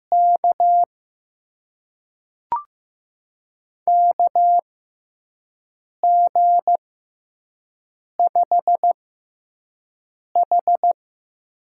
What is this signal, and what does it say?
Morse code, heard as a keyed tone, about 15 wpm, spaced as for 5 wpm, 700 Hz. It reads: K KG5H
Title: USA Call Signs - 6 Characters - 6x Character Spacing - 15wpm